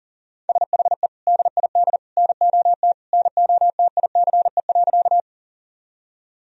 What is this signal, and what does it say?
Morse code, heard as a keyed tone, about 40 wpm, 700 Hz. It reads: SHE DID NOT NOTICE.
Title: Sentences from Top 1000 Words - Review Part 2 - 40wpm